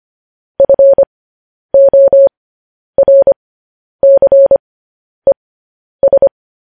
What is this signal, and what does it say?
Morse code, heard as a keyed tone, about 25 wpm, spaced as for 10 wpm, 550 Hz. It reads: FORCES
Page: news headlines in morse code at 10 wpm